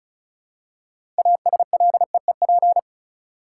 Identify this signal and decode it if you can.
Morse code, heard as a keyed tone, about 36 wpm, 700 Hz. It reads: ASLEEP
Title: Six Letter Words - 35wpm